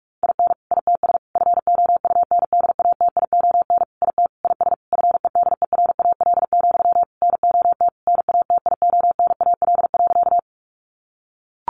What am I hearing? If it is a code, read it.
Morse code, heard as a keyed tone, about 44 wpm, 700 Hz. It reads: IN ITS FOUNDATION IT IS FEDERAL, NOT NATIONAL.
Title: Sentences from Top 1200 Words 45wpm